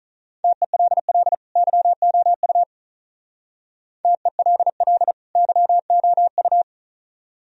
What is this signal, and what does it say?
Morse code, heard as a keyed tone, about 43 wpm, 700 Hz. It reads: TELL YOU TELL YOU
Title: Speed-Racing - Sets of 2 Words 35wpm